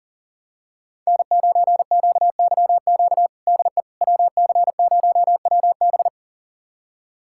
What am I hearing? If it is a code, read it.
Morse code, heard as a keyed tone, about 40 wpm, 700 Hz. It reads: N9QYQ DE WC0WB
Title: USA Callsign DE Callsign - 5 Characters - 40wpm